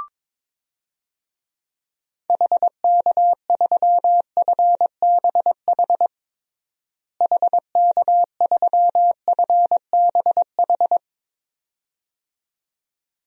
Morse code, heard as a keyed tone, about 22 wpm, 700 Hz. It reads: HK3FBH HK3FBH
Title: DXCC-100 Call Signs - 6 Characters - Repeated 2x - 22wpm